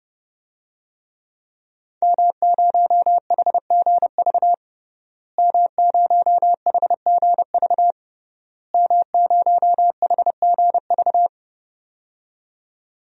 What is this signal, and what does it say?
Morse code, heard as a keyed tone, about 32 wpm, 700 Hz. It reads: M0HGV M0HGV M0HGV
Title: DXCC-100 Call Signs - 5 Characters - Repeated 3x - 30wpm